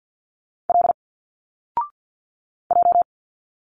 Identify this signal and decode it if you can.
Morse code, heard as a keyed tone, about 51 wpm, 700 Hz. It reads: L J
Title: Single Letters 50wpm